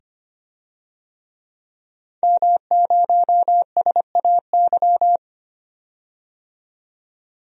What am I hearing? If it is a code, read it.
Morse code, heard as a keyed tone, about 25 wpm, 690 Hz. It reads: M0SAY
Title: DXCC-100 Call Signs - 5 Characters - 25wpm